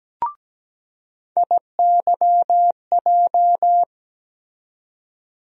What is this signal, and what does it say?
Morse code, heard as a keyed tone, about 17 wpm, 700 Hz. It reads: IYJ
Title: USA Call Sign Postfixes - 3 Letters 17wpm